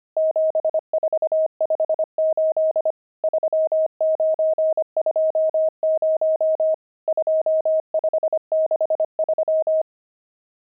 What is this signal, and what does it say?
Morse code, heard as a keyed tone, about 25 wpm, 630 Hz. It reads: 7458 3920 2563
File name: pangram-fs-25-cs-25-nums13